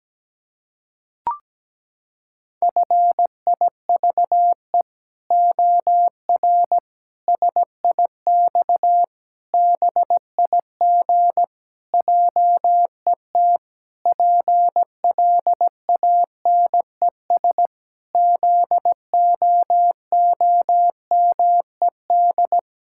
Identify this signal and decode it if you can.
Morse code, heard as a keyed tone, about 17 wpm, 700 Hz. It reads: FIVE OR SIX BIG JET PLANES ZOOMED
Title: Pangram Challenge v2 - 17wpm